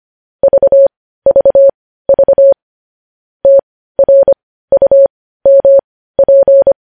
Morse code, heard as a keyed tone, about 25 wpm, spaced as for 15 wpm, 550 Hz. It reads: VVV TRUMP
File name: podcast / News headlines-1x-25x15